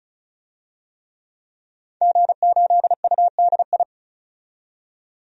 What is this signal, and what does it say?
Morse code, heard as a keyed tone, about 36 wpm, 700 Hz. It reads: G8UDI
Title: DXCC-100 Call Signs - 5 Characters - 35wpm